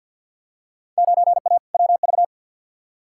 Morse code, heard as a keyed tone, about 50 wpm, 700 Hz. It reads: 9A WV
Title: Field Day Exchanges - 50wpm